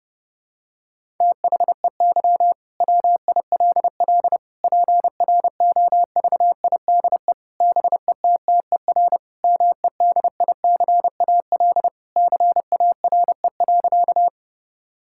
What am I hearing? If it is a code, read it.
Morse code, heard as a keyed tone, about 30 wpm, 700 Hz. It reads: THEY WILL PROVIDE BETTER MEDICAL CARE.
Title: Sentences from Top 1000 Words - Review Part 5 - 30wpm